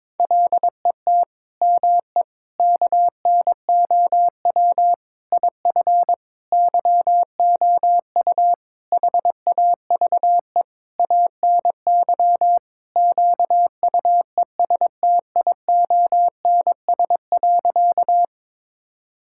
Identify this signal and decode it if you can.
Morse code, heard as a keyed tone, about 22 wpm, 700 Hz. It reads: LET ME KNOW IF YOU HAVE ANY QUESTIONS.
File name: practice / 22wpm / Sentences with 8-Word Phrases - 22wpm